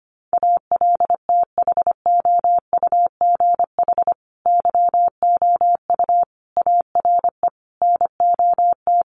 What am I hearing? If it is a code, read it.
Morse code, heard as a keyed tone, about 25 wpm, 700 Hz. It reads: ALTHOUGH YOU ARE NOT
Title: Sentences from Top 1100 Words 25wpm